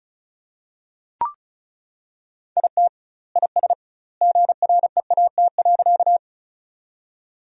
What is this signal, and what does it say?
Morse code, heard as a keyed tone, about 36 wpm, 700 Hz. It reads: IT IS GREAT.
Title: 2x Word-Spacing - Sentences from Top 200 Words - 35wpm